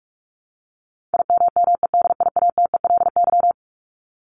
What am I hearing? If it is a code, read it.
Morse code, heard as a keyed tone, about 44 wpm, 700 Hz. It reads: IMMEDIATELY
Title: LY Suffix - Top 100 Words with Base Words 45wpm